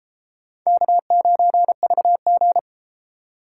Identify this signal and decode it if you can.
Morse code, heard as a keyed tone, about 31 wpm, 700 Hz. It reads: K9VG